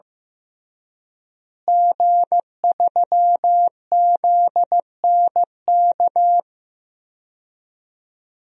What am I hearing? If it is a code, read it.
Morse code, heard as a keyed tone, about 15 wpm, 700 Hz. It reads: G3ZNK
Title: DXCC-100 Call Signs - 5 Characters - Repeated 3x - 15wpm